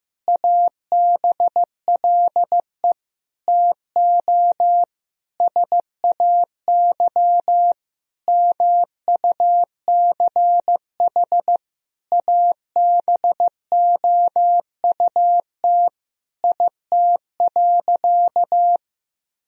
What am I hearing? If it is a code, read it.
Morse code, heard as a keyed tone, about 15 wpm, 700 Hz. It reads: ABLE TO SAY MUCH ABOUT IT.
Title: Sentences from Top 1000 Words - Review Part 2 - 15wpm